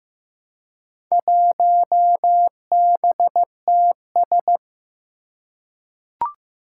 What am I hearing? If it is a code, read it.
Morse code, heard as a keyed tone, about 15 wpm, 700 Hz. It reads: 1BTS
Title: Four Challenging Letter-Number Combinations 15wpm